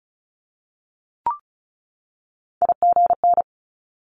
Morse code, heard as a keyed tone, about 36 wpm, 700 Hz. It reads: IGN
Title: Multi-letter Phonemes 35wpm